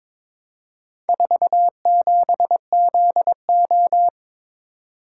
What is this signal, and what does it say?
Morse code, heard as a keyed tone, about 22 wpm, 700 Hz. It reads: 47ZO
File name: practice / 22wpm / Four Letter-Number Combinations - 22wpm